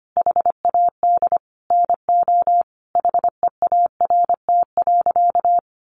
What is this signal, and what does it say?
Morse code, heard as a keyed tone, about 25 wpm, 700 Hz. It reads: HAD NO HEART.
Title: Sentences from Top 1200 Words 25wpm